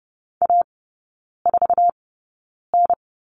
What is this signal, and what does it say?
Morse code, heard as a keyed tone, about 30 wpm, spaced as for 9 wpm, 700 Hz. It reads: A4N
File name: course / 30:10wpm / 087 Lesson 15e - Number 5 - Three Characters 30wpm